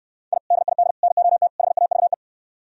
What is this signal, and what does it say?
Morse code, heard as a keyed tone, about 67 wpm, 700 Hz. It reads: I DID NOT HAVE